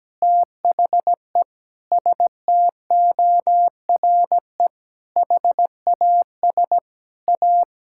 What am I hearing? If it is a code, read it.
Morse code, heard as a keyed tone, about 17 wpm, 700 Hz. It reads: THE STORE HAS A
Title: Sentences from Top 1000 Words - Review Part 6 - 17wpm